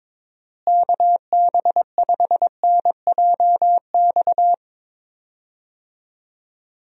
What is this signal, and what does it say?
Morse code, heard as a keyed tone, about 22 wpm, 700 Hz. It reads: KB5NJX
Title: USA Call Signs - 6 Characters - Repeated 2x - 22wpm